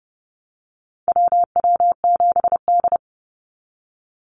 Morse code, heard as a keyed tone, about 30 wpm, 700 Hz. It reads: WW7D
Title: NA SOTA Activator Callsigns 30wpm